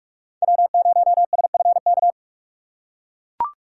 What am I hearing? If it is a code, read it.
Morse code, heard as a keyed tone, about 44 wpm, 700 Hz. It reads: W0SFK
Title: USA Call Signs - 6x Character Spacing - 45wpm